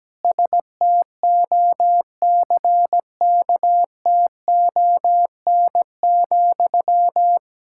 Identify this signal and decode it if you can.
Morse code, heard as a keyed tone, about 17 wpm, 700 Hz. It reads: STOCKTON,
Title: Practice QSO - Exchange 1 - Beginning - 17wpm